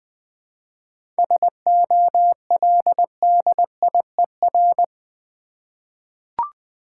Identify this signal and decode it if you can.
Morse code, heard as a keyed tone, about 20 wpm, 700 Hz. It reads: SOLDIER
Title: 7 Letter Words - First 3 Letters Given - 20wpm